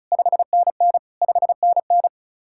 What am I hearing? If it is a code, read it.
Morse code, heard as a keyed tone, about 44 wpm, 700 Hz. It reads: 5NN 5NN